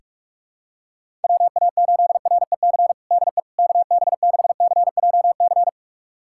Morse code, heard as a keyed tone, about 45 wpm, 700 Hz. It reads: WA8REC DE KB6CJC
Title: USA Callsign DE Callsign - 6 Characters - 45wpm